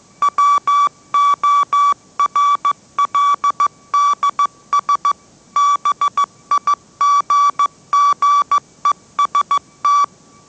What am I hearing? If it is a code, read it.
Morse code, heard as a keyed tone, about 15 wpm, 1150 Hz. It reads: WORLDS BIGGEST
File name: mrx / mrx065 16wpm